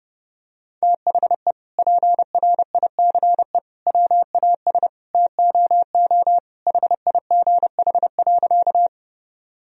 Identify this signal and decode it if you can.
Morse code, heard as a keyed tone, about 32 wpm, 700 Hz. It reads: THE PRICE WAS TOO HIGH.
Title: Sentences from Top 1000 Words - Review Part 4 - 30wpm